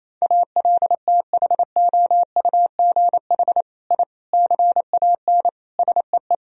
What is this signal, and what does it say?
Morse code, heard as a keyed tone, about 28 wpm, 700 Hz. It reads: ALTHOUGH I CAN SEE
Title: Sentences from Top 1000 Words - Review Part 3 - 28wpm